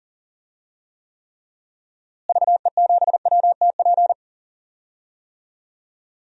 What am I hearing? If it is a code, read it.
Morse code, heard as a keyed tone, about 40 wpm, 700 Hz. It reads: VE7WTP
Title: DXCC-100 Call Signs - 6 Characters - Repeated 3x - 40wpm